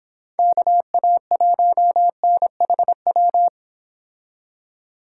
Morse code, heard as a keyed tone, about 26 wpm, 700 Hz. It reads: KA1NHW